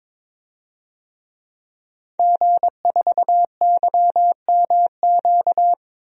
Morse code, heard as a keyed tone, about 22 wpm, 700 Hz. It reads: G4YMQ
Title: DXCC-100 Call Signs - 5 Characters - 22wpm